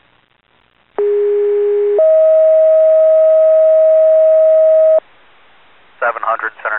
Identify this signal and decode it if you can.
Two-tone page, heard as a one-second tone, then a three-second tone, about 410 then 640 Hz, with speech. It is heard from 1.0 to 5.0 s.